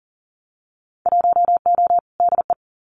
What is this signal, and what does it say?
Morse code, heard as a keyed tone, about 38 wpm, 700 Hz. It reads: 1O DE
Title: Winter Field Day Exchanges 40wpm